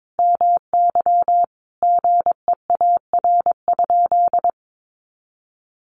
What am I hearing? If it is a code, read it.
Morse code, heard as a keyed tone, about 22 wpm, 700 Hz. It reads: MY GEAR?
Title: Sentences from Top 1400 Words 22wpm